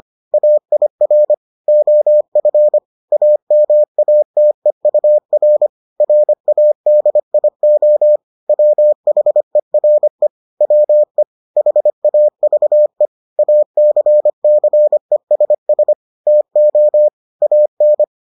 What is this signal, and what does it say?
Morse code, heard as a keyed tone, about 25 wpm, 600 Hz. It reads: AIR OF AMATEUR RADIO WHERE WE HAVE ACCESS TO AN